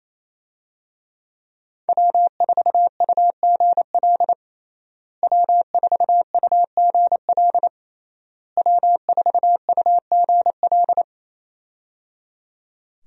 Morse code, heard as a keyed tone, about 28 wpm, 700 Hz. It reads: W4UGL W4UGL W4UGL
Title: USA Call Signs - 5 Characters - Repeated 3x - 28wpm